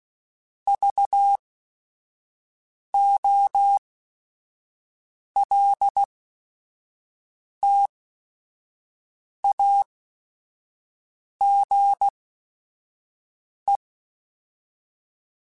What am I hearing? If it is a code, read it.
Morse code, heard as a keyed tone, about 16 wpm, spaced as for 5 wpm, 800 Hz. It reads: VOLTAGE
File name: Technical Subjects / Technical Subjects-mp3s / XC @5